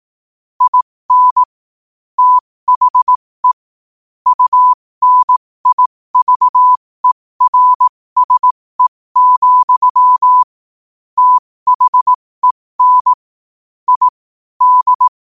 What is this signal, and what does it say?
Morse code, heard as a keyed tone, about 18 wpm, spaced as for 15 wpm, 1000 Hz. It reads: IN THE UNIVERSE, THEN I D
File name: qotdfiles / 15wpm